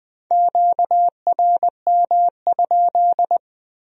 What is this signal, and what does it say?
Morse code, heard as a keyed tone, about 20 wpm, 700 Hz. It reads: QRM?